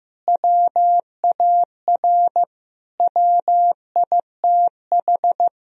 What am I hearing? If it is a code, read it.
Morse code, heard as a keyed tone, about 15 wpm, 700 Hz. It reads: WAR WITH
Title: Sentences from Top 1000 Words - Review Part 2 - 15wpm